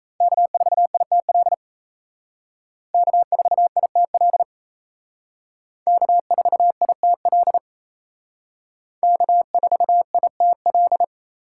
Morse code, heard as a keyed tone, about 36 wpm, 700 Hz. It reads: K4ITL K4ITL K4ITL K4ITL